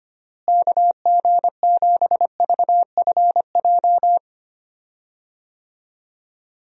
Morse code, heard as a keyed tone, about 25 wpm, 700 Hz. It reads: KG7VFJ
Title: USA Call Signs - 6 Characters - Repeated 3x - 25wpm